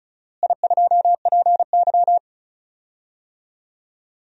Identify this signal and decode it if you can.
Morse code, heard as a keyed tone, about 34 wpm, 700 Hz. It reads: I2PY
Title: DXCC-100 Call Signs - 4 Characters - Repeated 3x - 35wpm